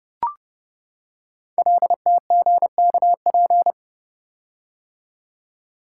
Morse code, heard as a keyed tone, about 30 wpm, 700 Hz. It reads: LTGKP
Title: Code Groups - 5 Letter:Numbers - 30wpm